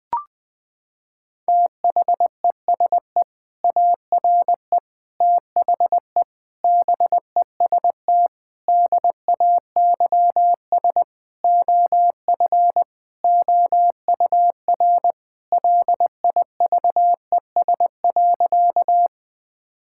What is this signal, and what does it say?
Morse code, heard as a keyed tone, about 20 wpm, 700 Hz. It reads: THESE ARE THE BEST DAYS OF OUR LIVES.
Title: Sentences with 2-Word Phrases - 20wpm